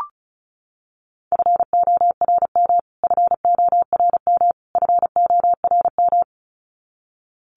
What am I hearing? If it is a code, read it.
Morse code, heard as a keyed tone, about 34 wpm, 700 Hz. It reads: FORM FORM FORM
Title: Top 500 Words Repeated 35wpm